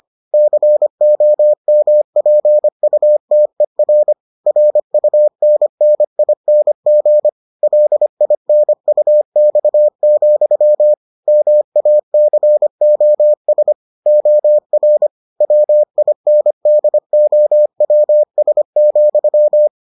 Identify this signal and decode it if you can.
Morse code, heard as a keyed tone, about 25 wpm, 600 Hz. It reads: COMPUTER RUNNING LINUX, MACOS OR WINDOWS,